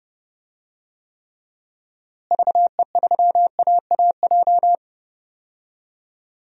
Morse code, heard as a keyed tone, about 30 wpm, 700 Hz. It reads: VE3AAJ
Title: DXCC-100 Call Signs - 6 Characters - Repeated 3x - 30wpm